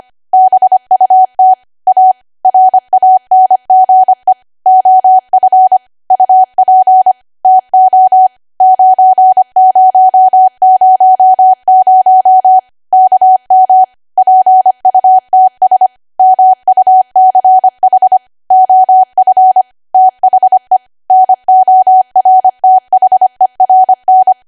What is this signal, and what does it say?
Morse code, heard as a keyed tone, about 25 wpm, 750 Hz. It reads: BUT A RANGE OF UP TO 9000 KM PUTS MUCH OF THE NORTHERN
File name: cwa / W1AW-Level3 / 25WPM